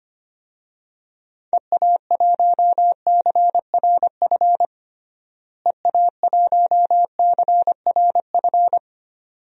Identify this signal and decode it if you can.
Morse code, heard as a keyed tone, about 25 wpm, 700 Hz. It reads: EA1CRF EA1CRF
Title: DXCC-100 Call Signs - 6 Characters - Repeated 2x - 25wpm